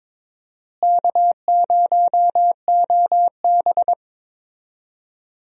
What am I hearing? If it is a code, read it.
Morse code, heard as a keyed tone, about 23 wpm, 700 Hz. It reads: K0OB
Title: USA Call Signs - 4 Characters - Repeated 3x - 22wpm